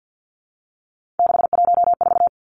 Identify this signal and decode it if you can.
Morse code, heard as a keyed tone, about 50 wpm, 700 Hz. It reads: 614
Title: Three Challenging Letter-Number Combinations 50wpm